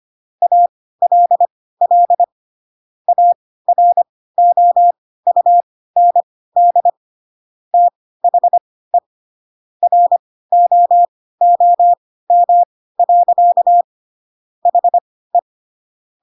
Morse code, heard as a keyed tone, about 25 wpm, spaced as for 16 wpm, 700 Hz. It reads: ALL AROUND THE ROOM. HE
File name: Pinocchio / Pinocchio ch3 10 25 ik1hge